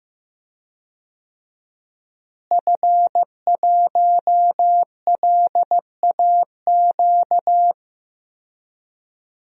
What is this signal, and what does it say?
Morse code, heard as a keyed tone, about 15 wpm, 700 Hz. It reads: F1LAQ
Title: DXCC-100 Call Signs - 5 Characters - Repeated 2x - 15wpm